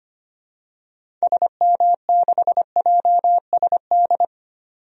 Morse code, heard as a keyed tone, about 25 wpm, 700 Hz. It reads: SM6JSD
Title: DXCC-100 Call Signs - 6 Characters - 25wpm